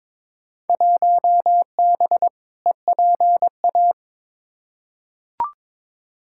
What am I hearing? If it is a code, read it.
Morse code, heard as a keyed tone, about 22 wpm, 700 Hz. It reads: 1B EPA
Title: Field Day Exchanges Encore - 22wpm